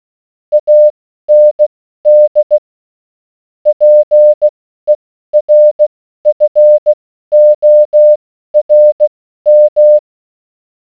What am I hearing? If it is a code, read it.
Morse code, heard as a keyed tone, about 16 wpm, spaced as for 12 wpm, 600 Hz. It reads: AND PERFORM